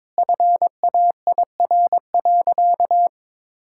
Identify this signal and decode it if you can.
Morse code, heard as a keyed tone, about 22 wpm, 700 Hz. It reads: FAIR.
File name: practice / 22wpm / Sentences with 8-Word Phrases - 22wpm